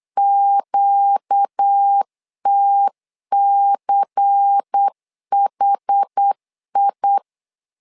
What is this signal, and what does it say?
Morse code, heard as a keyed tone, about 8 wpm, 800 Hz. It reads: QTCHI
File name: LEZIONI / 5 LBVC / 5 LBVC 8wpm 2935